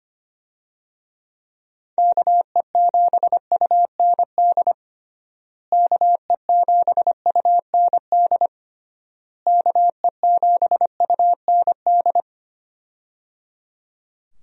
Morse code, heard as a keyed tone, about 25 wpm, 700 Hz. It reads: KE7UND KE7UND KE7UND
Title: USA Call Signs - 6 Characters - Repeated 3x - 25wpm